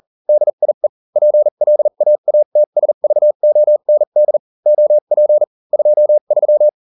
Morse code, heard as a keyed tone, about 40 wpm, spaced as for 35 wpm, 600 Hz. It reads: DIE PLAATSVOND OP 23